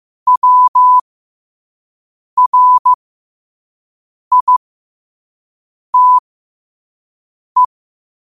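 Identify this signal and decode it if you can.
Morse code, heard as a keyed tone, about 15 wpm, spaced as for 5 wpm, 1000 Hz. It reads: WRITE
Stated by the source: Morse code quote of the day at 5 wpm.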